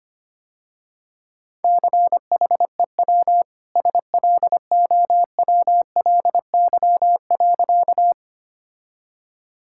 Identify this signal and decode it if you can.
Morse code, heard as a keyed tone, about 25 wpm, 700 Hz. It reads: CHEW SLOWLY.
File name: Easy Sentences with Hints - 25wpm